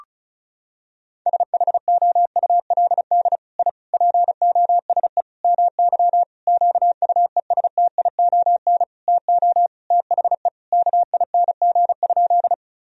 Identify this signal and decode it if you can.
Morse code, heard as a keyed tone, about 36 wpm, 700 Hz. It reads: SHOULD I POSE MY QUESTION TO THE KING?
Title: Sentences from Top 1000 Words - Review Part 2 - 35wpm